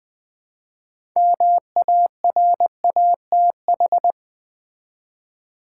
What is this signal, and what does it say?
Morse code, heard as a keyed tone, about 20 wpm, 700 Hz. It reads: MARATH
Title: Build A Word - 2 Letters At A Time - 8-Letter Words - 20wpm